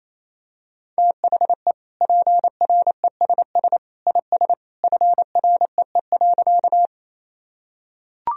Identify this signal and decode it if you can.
Morse code, heard as a keyed tone, about 28 wpm, 700 Hz. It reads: THE PRESS IS FREE.
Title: Sentences from Top 1000 Words - Review Part 1 - 28wpm